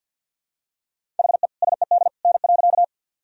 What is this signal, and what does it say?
Morse code, heard as a keyed tone, about 51 wpm, 700 Hz. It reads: HE SED N.
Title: Sentences from Top 100 Words and Abbreviations 50wpm